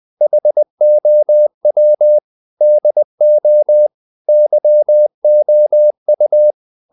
Morse code, heard as a keyed tone, about 20 wpm, 600 Hz. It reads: HOW DO YOU
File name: the old man and the sea / the old man and the sea part 1 20wpm